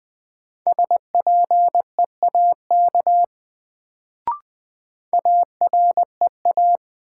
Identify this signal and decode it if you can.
Morse code, heard as a keyed tone, about 20 wpm, 700 Hz. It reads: SPEAK AREA
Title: Sentences from Top 1000 Words - Review Part 2 - New Word Drill - 20wpm